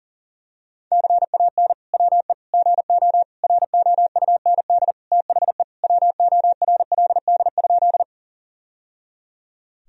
Morse code, heard as a keyed tone, about 40 wpm, 700 Hz. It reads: CAN WE GO ROUND THE WORLD?
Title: Sentences from Top 1000 Words - Review Part 1 - 40wpm